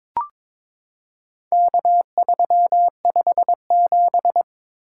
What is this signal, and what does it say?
Morse code, heard as a keyed tone, about 21 wpm, 700 Hz. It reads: K357
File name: practice / 22wpm / Four Letter-Number Combinations - 22wpm